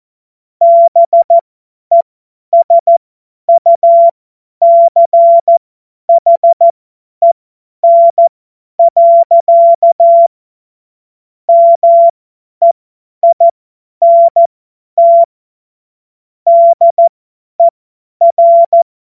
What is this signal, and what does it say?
Morse code, heard as a keyed tone, about 14 wpm, spaced as for 10 wpm, 680 Hz. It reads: BESUCHEN. MEINT DER